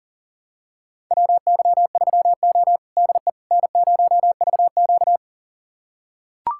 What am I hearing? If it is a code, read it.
Morse code, heard as a keyed tone, about 40 wpm, 700 Hz. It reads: WY3O DE N0VQ
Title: USA Callsign DE Callsign - 4 Characters - 40wpm